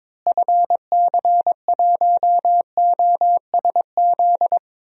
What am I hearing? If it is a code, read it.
Morse code, heard as a keyed tone, about 22 wpm, 700 Hz. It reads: FC1OSZ
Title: DXCC-100 Call Signs - 6 Characters - Repeated 3x - 22wpm